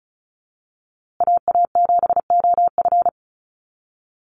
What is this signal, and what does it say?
Morse code, heard as a keyed tone, about 34 wpm, 700 Hz. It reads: AA7OF
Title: USA Call Signs Repeated 3x 35wpm